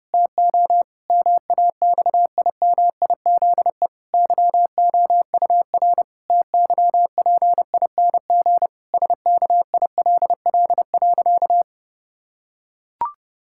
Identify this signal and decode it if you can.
Morse code, heard as a keyed tone, about 30 wpm, 700 Hz. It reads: TO MAXIMIZE YOUR TYPING SKILL.
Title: Pangram Challenge v2 - 30wpm